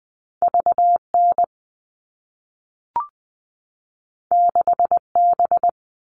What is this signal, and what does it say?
Morse code, heard as a keyed tone, about 20 wpm, 700 Hz. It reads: VN 6B
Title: All Cut Numbers - 2-Digit 20wpm